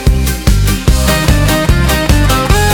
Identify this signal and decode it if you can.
There is no code.